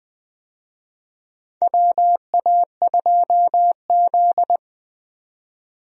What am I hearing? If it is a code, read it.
Morse code, heard as a keyed tone, about 20 wpm, 700 Hz. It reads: WA2Z
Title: USA Call Signs - 4 Characters - 20wpm